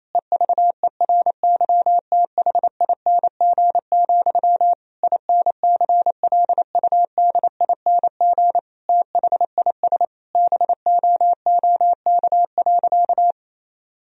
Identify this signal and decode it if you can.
Morse code, heard as a keyed tone, about 28 wpm, 700 Hz. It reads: EVERYTHING, INCLUDING THIS BOOK.
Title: Sentences from Top 1000 Words - Review Part 4 - 28wpm